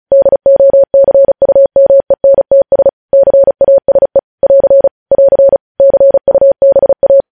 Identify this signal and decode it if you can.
Morse code, heard as a keyed tone, about 36 wpm, 550 Hz. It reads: DOCUMENTS CASE <AR> <AR> CUBA
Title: News Headlines at 35